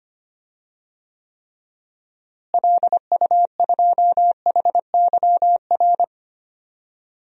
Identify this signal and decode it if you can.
Morse code, heard as a keyed tone, about 25 wpm, 700 Hz. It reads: LU2HYR
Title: DXCC-100 Call Signs - 6 Characters - Repeated 3x - 25wpm